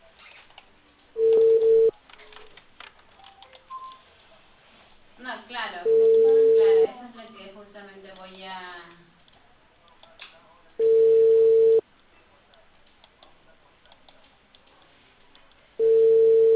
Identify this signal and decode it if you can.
Ringback tone (China, 450 Hz), with speech.